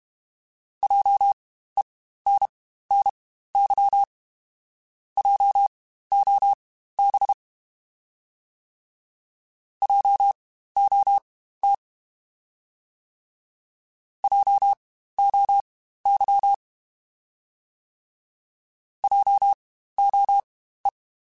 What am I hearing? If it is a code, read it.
Morse code, heard as a keyed tone, about 32 wpm, spaced as for 15 wpm, 800 Hz. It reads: JENNY JOB JOT JOY JOE
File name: Spaced Alphas / Spaced Alphas mp3s / Js sp @30